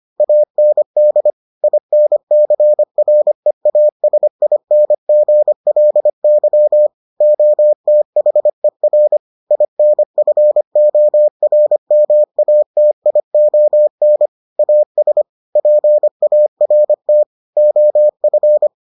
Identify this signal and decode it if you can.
Morse code, heard as a keyed tone, about 25 wpm, 600 Hz. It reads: AND INCREASINGLY OTHER INFORMATION AS PART OF